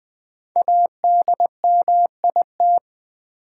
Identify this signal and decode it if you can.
Morse code, heard as a keyed tone, about 20 wpm, 700 Hz. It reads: ADMIT
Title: 5 Letter Words - First 3 Letters Given - 20wpm